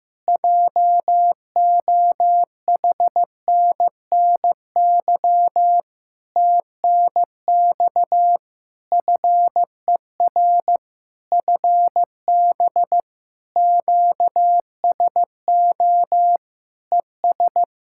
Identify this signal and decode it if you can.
Morse code, heard as a keyed tone, about 15 wpm, 700 Hz. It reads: JOHNNY TNX FER FB QSO ES
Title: Practice QSO - Exchange 4 - Ending - 15wpm